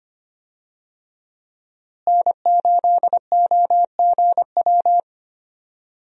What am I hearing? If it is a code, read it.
Morse code, heard as a keyed tone, about 25 wpm, 700 Hz. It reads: N8OGW
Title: USA Call Signs - 5 Characters - Repeated 2x - 25wpm